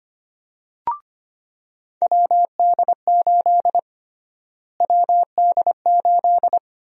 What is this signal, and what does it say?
Morse code, heard as a keyed tone, about 25 wpm, 700 Hz. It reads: WD8 WD8
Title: USA Call Sign Prefixes - Repeated 2x - 25wpm